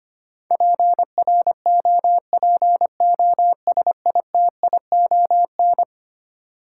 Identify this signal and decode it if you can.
Morse code, heard as a keyed tone, about 25 wpm, 700 Hz. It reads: PROPOSITION
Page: morse-code-ninja-practice-25wpm